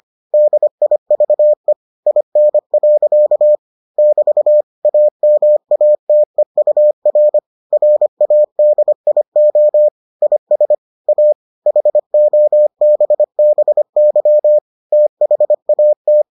Morse code, heard as a keyed tone, about 25 wpm, 600 Hz. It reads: DIVE IN. <BT> AMATEUR RADIO IS A HOBBY THAT